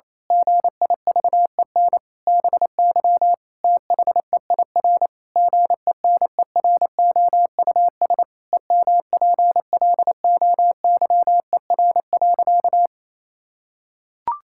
Morse code, heard as a keyed tone, about 28 wpm, 700 Hz. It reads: GIVEN BY THEIR GENEROUS EMPLOYER.
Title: Pangram Challenge v2 - 28wpm